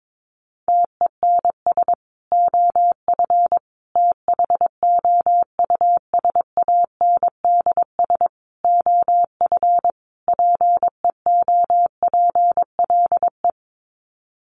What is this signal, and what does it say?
Morse code, heard as a keyed tone, about 22 wpm, 700 Hz. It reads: TENS OF THOUSANDS OF PEOPLE
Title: Sets of 5 Words Encore2 22wpm